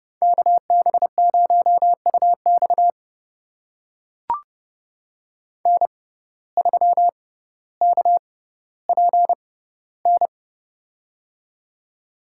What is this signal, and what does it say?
Morse code, heard as a keyed tone, about 30 wpm, spaced as for 15 wpm, 700 Hz. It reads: KB0UX N3KPN
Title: USA Call Signs - 5 Characters - 6x Character Spacing - 30wpm